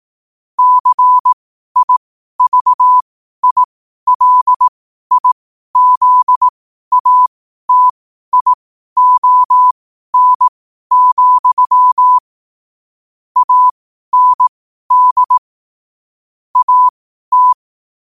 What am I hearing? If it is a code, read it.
Morse code, heard as a keyed tone, about 18 wpm, spaced as for 12 wpm, 1000 Hz. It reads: CIVILIZATION, AND AT
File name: qotdfiles / 12wpm